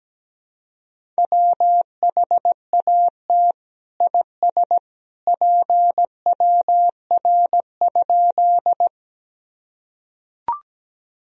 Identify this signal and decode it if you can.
Morse code, heard as a keyed tone, about 17 wpm, 700 Hz. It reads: WHAT IS PWR?
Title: Sentences from Top 500 Words and Abbreviations 17wpm